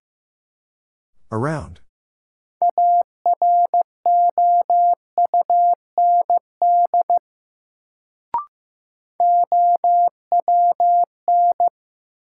Morse code, heard as a keyed tone, about 16 wpm, 700 Hz. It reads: AROUND OWN
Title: Sets of 2 Words - New Word Drill - 15wpm